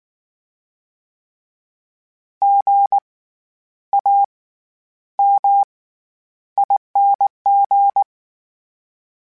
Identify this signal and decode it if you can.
Morse code, heard as a keyed tone, about 19 wpm, spaced as for 10 wpm, 800 Hz. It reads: GAMING
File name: Special INGs / Special INGs-mp3s / SlowToFaster ing @18